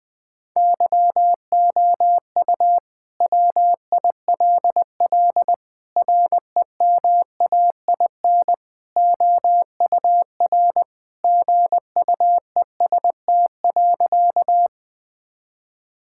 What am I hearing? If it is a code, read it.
Morse code, heard as a keyed tone, about 20 wpm, 700 Hz. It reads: YOU WILL REMAIN OUR GUEST.
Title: Sentences from Top 1600 Words 20wpm